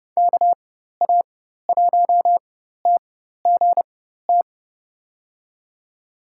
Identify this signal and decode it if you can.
Morse code, heard as a keyed tone, about 30 wpm, spaced as for 14 wpm, 700 Hz. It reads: KA1TGT
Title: USA Call Signs - 6 Characters - 4x Character Spacing - 30wpm